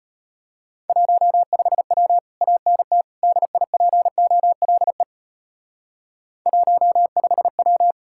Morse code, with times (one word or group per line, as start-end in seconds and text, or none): 0.9-2.2 15W
2.4-3.0 ANT
3.2-5.0 DIPOLE
6.5-7.9 15W